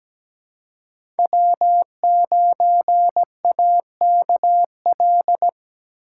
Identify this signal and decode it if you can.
Morse code, heard as a keyed tone, about 17 wpm, 700 Hz. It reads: W9AKL
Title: USA Call Signs - 5 Characters - 4x Character Spacing - 17wpm